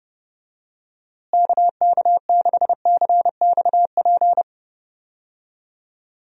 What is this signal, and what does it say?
Morse code, heard as a keyed tone, about 31 wpm, 700 Hz. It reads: KK6CXP
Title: USA Call Signs - 6 Characters - Repeated 2x - 30wpm